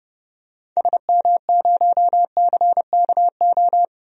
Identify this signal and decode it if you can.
Morse code, heard as a keyed tone, about 30 wpm, 700 Hz. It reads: SM0CKO
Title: DXCC-100 Call Signs - 6 Characters - 30wpm